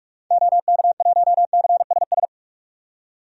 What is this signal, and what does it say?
Morse code, heard as a keyed tone, about 43 wpm, 690 Hz. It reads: OK1CSS
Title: DXCC-100 Call Signs - 6 Characters - 45wpm